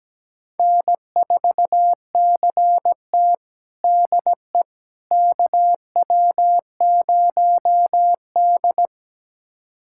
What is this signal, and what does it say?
Morse code, heard as a keyed tone, about 17 wpm, 700 Hz. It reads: N4CT DE KW0D
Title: USA Callsign DE Callsign - 4 Characters - 17wpm